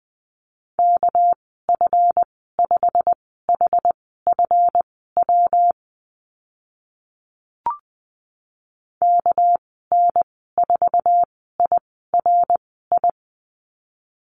Morse code, heard as a keyed tone, about 20 wpm, spaced as for 14 wpm, 700 Hz. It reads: KF5HFW KN4IRI
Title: 207 Lesson 33g - Letter j - Call Signs 20wpm